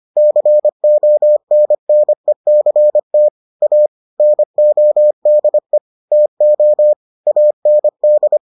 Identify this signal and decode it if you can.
Morse code, heard as a keyed tone, about 25 wpm, 600 Hz. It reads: CONNECT A NODE TO AND